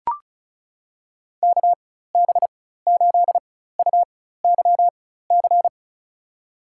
Morse code, heard as a keyed tone, about 36 wpm, spaced as for 16 wpm, 700 Hz. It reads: KB8UYC
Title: USA Call Signs - 6 Characters - 4x Character Spacing - 35wpm